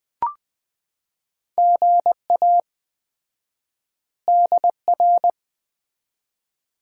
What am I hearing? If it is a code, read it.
Morse code, heard as a keyed tone, about 20 wpm, 700 Hz. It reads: GA DR
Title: Common QSO Phrases - 4x Word-Spacing - 20wpm